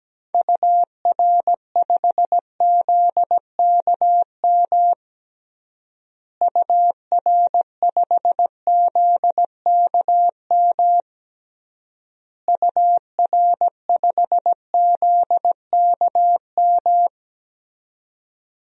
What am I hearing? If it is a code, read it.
Morse code, heard as a keyed tone, about 17 wpm, 700 Hz. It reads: UR5ZKM UR5ZKM UR5ZKM
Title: DXCC-100 Call Signs - 6 Characters - Repeated 3x - 17wpm